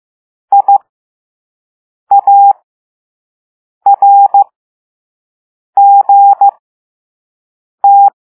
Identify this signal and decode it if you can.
Morse code, heard as a keyed tone, about 15 wpm, spaced as for 5 wpm, 800 Hz. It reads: IARGT